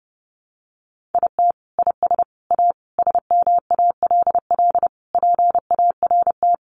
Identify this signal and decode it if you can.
Morse code, heard as a keyed tone, about 30 wpm, 700 Hz. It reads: IT IS A SMALL PART
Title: Sentences from Top 1500 Words 30wpm